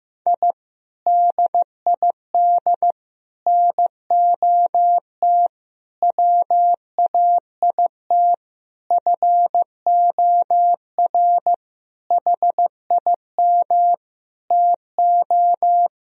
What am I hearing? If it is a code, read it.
Morse code, heard as a keyed tone, about 15 wpm, 700 Hz. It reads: I DID NOT WAIT FOR HIM TO
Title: Sentences from Top 1000 Words - Review Part 2 - 15wpm